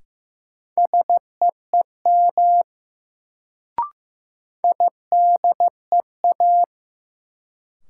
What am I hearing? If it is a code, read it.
Morse code, heard as a keyed tone, about 15 wpm, 700 Hz. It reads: SEEM IDEA
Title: Sentences from Top 300 Words - New Word Drill - 15wpm